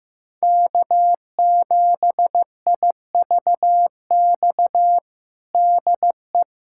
Morse code, heard as a keyed tone, about 15 wpm, 700 Hz. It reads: K7IVX DE
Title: USA Callsign DE Callsign - 5 Characters - 15wpm